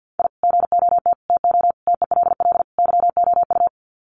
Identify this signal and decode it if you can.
Morse code, heard as a keyed tone, about 50 wpm, 700 Hz. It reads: I GOT TO TELL YOU
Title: Sets of 5 Words Encore2 50wpm